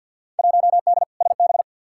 Morse code, heard as a keyed tone, about 51 wpm, 700 Hz. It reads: 1D SB
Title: Field Day Exchanges - 50wpm